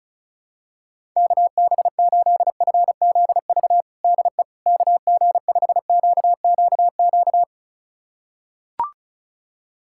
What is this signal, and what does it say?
Morse code, heard as a keyed tone, about 36 wpm, 700 Hz. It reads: KB8FZV DE KG5QQQ
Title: USA Callsign DE Callsign - 6 Characters - 35wpm